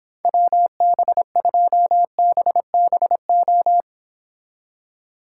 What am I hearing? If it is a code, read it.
Morse code, heard as a keyed tone, about 26 wpm, 700 Hz. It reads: WB2BBO